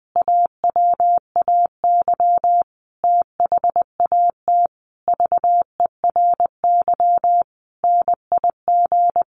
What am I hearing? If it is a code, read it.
Morse code, heard as a keyed tone, about 19 wpm, 700 Hz. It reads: AWAY THAT VERY NIG
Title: Sentences from Top 1200 Words 20wpm